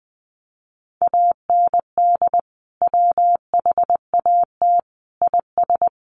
Morse code, heard as a keyed tone, about 20 wpm, 700 Hz. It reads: AND WHAT IS
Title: Sentences from Top 1500 Words 20wpm